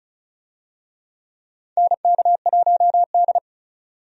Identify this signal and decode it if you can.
Morse code, heard as a keyed tone, about 34 wpm, 700 Hz. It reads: NK1D
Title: USA Call Signs - 4 Characters - 35wpm